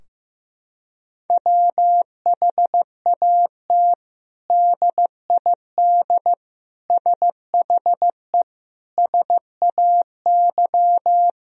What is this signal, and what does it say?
Morse code, heard as a keyed tone, about 15 wpm, 700 Hz. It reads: WHAT DID SHE SAY
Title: Sentences from Top 1000 Words - Review Part 2 - 15wpm